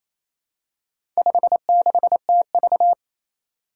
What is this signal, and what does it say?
Morse code, heard as a keyed tone, about 28 wpm, 700 Hz. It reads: 56TV
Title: Four Challenging Letter-Number Combinations 28wpm